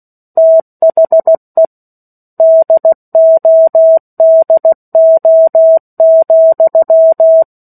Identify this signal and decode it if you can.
Morse code, heard as a keyed tone, about 16 wpm, 650 Hz. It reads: THE DODO,
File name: 03-Chapter-03 16 WPM